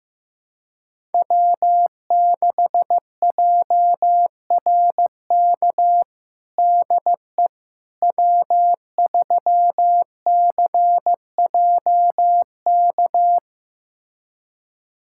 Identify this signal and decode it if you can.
Morse code, heard as a keyed tone, about 15 wpm, 700 Hz. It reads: W6JRK DE W3CJK
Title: USA Callsign DE Callsign - 5 Characters - 15wpm